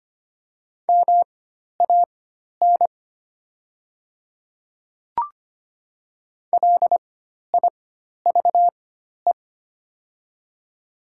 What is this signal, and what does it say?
Morse code, heard as a keyed tone, about 25 wpm, spaced as for 12 wpm, 700 Hz. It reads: MAN LIVE
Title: 4x Character Spacing - Top 200 Words - 25wpm